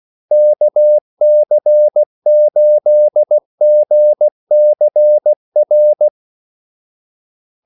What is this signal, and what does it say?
Morse code, heard as a keyed tone, about 16 wpm, 600 Hz. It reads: KC8GCR
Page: Kaldesignaler 16 wpm